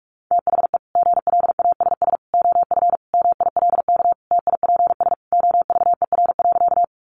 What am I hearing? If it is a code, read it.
Morse code, heard as a keyed tone, about 44 wpm, 700 Hz. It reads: THE GLASS OF MILK TIPS OVER.
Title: Additional Easy Phrases 45wpm